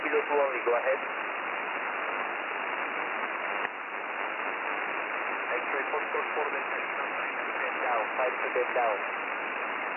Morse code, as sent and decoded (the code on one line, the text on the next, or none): none